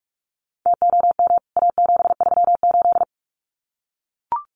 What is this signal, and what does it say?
Morse code, heard as a keyed tone, about 44 wpm, 700 Hz. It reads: TOM A738
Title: CWT Exchange 45wpm